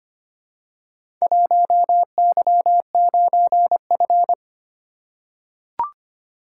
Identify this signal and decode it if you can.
Morse code, heard as a keyed tone, about 25 wpm, 700 Hz. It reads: 1Y9F
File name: practice / 25wpm / Four Letter-Number Combinations - 25wpm